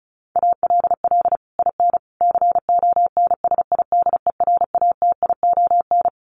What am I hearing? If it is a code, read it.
Morse code, heard as a keyed tone, about 34 wpm, 700 Hz. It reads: ALL IN CONSIDERATION
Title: Sentences from Top 1500 Words 35wpm